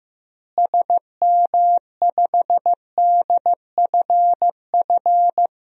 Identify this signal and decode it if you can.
Morse code, heard as a keyed tone, about 15 wpm, 700 Hz. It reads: SM5DFF
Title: DXCC-100 Call Signs - 6 Characters - Repeated 2x - 15wpm